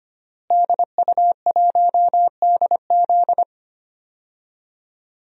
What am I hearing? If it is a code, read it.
Morse code, heard as a keyed tone, about 25 wpm, 700 Hz. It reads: DU1DZ
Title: DXCC-100 Call Signs - 5 Characters - Repeated 3x - 25wpm